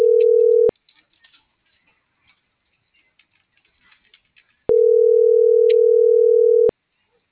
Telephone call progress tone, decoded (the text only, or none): ringback tone